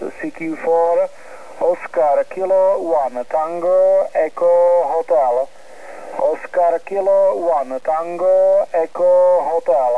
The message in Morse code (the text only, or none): none